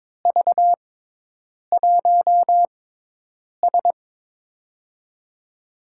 Morse code, heard as a keyed tone, about 22 wpm, spaced as for 8 wpm, 700 Hz. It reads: V1S
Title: USA Call Signs - 4 Characters - 6x Character Spacing - 22wpm